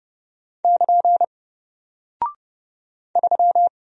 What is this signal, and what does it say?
Morse code, heard as a keyed tone, about 30 wpm, 700 Hz. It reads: <KN> 3
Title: Single Character (Letter:Number:Punctuation:Prosign) Alt 30wpm